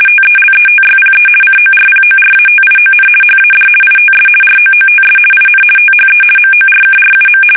RTTY, about 50 baud. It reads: TELETYPE CORPORATION HAD ITS ROOTS IN THE MORKRUM
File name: fsk850-50B / teletype-corp